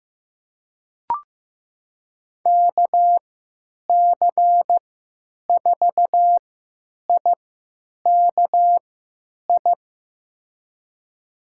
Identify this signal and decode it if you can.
Morse code, heard as a keyed tone, about 15 wpm, spaced as for 9 wpm, 700 Hz. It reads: KC4IKI
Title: USA Call Signs - 6 Characters - 3x Character Spacing - 15wpm